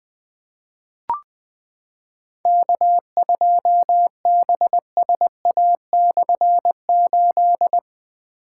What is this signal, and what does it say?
Morse code, heard as a keyed tone, about 20 wpm, 700 Hz. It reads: K2BSA/8
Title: K1USN SST QSO Practice - 20wpm